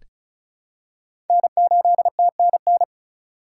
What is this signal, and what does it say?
Morse code, heard as a keyed tone, about 34 wpm, 700 Hz. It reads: N8TNN
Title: USA Call Signs - 5 Characters - 4x Character Spacing - 35wpm